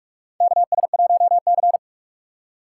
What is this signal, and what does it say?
Morse code, heard as a keyed tone, about 44 wpm, 700 Hz. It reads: KS1C
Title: USA Call Signs - 4 Characters - 6x Character Spacing - 45wpm